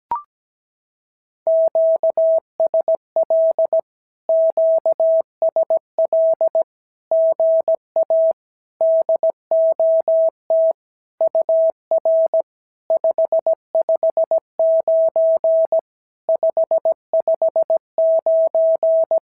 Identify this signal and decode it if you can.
Morse code, heard as a keyed tone, about 17 wpm, 650 Hz. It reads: QSL QSL GA DOT UR 559 559
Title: SOTA QSO Practice 17wpm